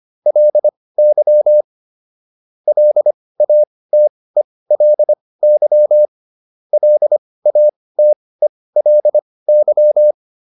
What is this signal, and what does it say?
Morse code, heard as a keyed tone, about 25 wpm, spaced as for 18 wpm, 600 Hz. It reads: LY LATELY LATELY